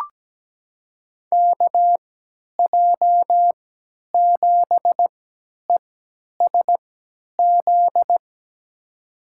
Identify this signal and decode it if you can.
Morse code, heard as a keyed tone, about 17 wpm, spaced as for 10 wpm, 700 Hz. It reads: KJ7ESZ
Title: USA Call Signs - 3x Character Spacing - 17wpm